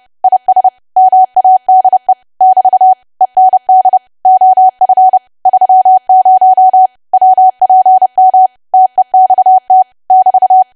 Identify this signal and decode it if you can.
Morse code, heard as a keyed tone, about 30 wpm, 750 Hz. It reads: IS MADE <BT> END OF 30 WPM TEXT <BT>
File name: cwa / W1AW-Level3 / 30WPM